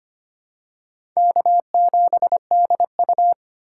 Morse code, heard as a keyed tone, about 25 wpm, 700 Hz. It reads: K7DU
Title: USA Call Signs - 4 Characters - 2x Character Spacing - 25wpm